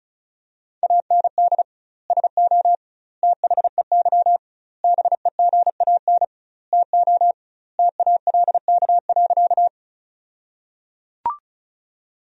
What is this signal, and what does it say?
Morse code, heard as a keyed tone, about 36 wpm, 700 Hz. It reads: AND SO THEY BEGAN TO TALK.
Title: 2x Word-Spacing - Sentences from Top 400 Words - 35wpm